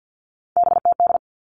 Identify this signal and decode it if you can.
Morse code, heard as a keyed tone, about 50 wpm, 700 Hz. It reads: BTD
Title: All Cut Numbers - 3-Digit 50wpm